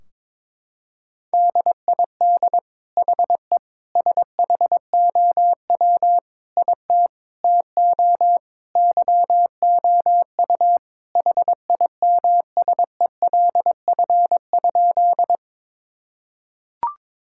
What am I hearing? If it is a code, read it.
Morse code, heard as a keyed tone, about 22 wpm, 700 Hz. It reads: DID HE SHOW IT TO YOU HIMSELF?
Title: Sentences from Top 1600 Words 22wpm